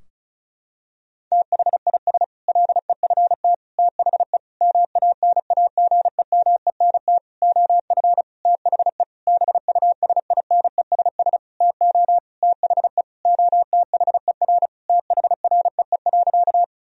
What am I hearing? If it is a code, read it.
Morse code, heard as a keyed tone, about 34 wpm, 700 Hz. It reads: THIS LEFT THE MANAGEMENT OF THE BUSINESS TO THE OTHER THREE.
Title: Sentences from Top 1000 Words - Review Part 4 - 35wpm